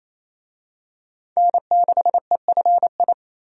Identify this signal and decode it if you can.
Morse code, heard as a keyed tone, about 28 wpm, 700 Hz. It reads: N6EFI
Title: USA Call Signs - 5 Characters - 28wpm